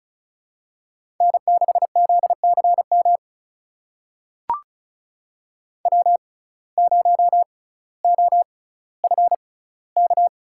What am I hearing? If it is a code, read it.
Morse code, heard as a keyed tone, about 36 wpm, spaced as for 18 wpm, 700 Hz. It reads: N6ZCM W0OFK
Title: USA Call Signs - 5 Characters - 6x Character Spacing - 35wpm